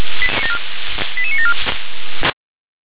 Five-tone selective call (ZVEI), heard as quick runs of five tones, tones of about 70 ms. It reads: 88974